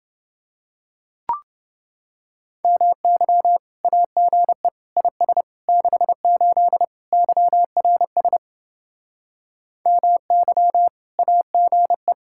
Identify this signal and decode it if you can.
Morse code, heard as a keyed tone, about 31 wpm, 700 Hz. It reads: MY AGE IS 68 YRS MY AGE